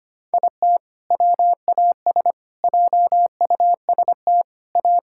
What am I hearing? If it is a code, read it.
Morse code, heard as a keyed tone, about 25 wpm, 700 Hz. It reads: IT WAS JUST A
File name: practice / 25wpm / Sentences with 3-Word Phrases - 25wpm